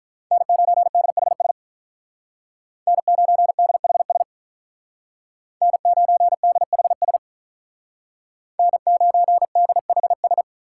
Morse code, heard as a keyed tone, about 43 wpm, 700 Hz. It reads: N9DHS N9DHS N9DHS N9DHS
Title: Speed Racing - USA Call Signs 35wpm